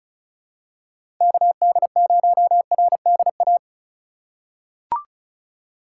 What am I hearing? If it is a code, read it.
Morse code, heard as a keyed tone, about 36 wpm, 700 Hz. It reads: KD0RDA
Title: USA Call Signs - 2x Character Spacing - 35wpm